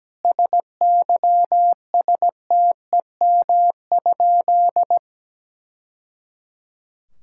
Morse code, heard as a keyed tone, about 17 wpm, 700 Hz. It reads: SYSTEM?
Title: Sentences from Top 1000 Words - Review Part 5 - 17wpm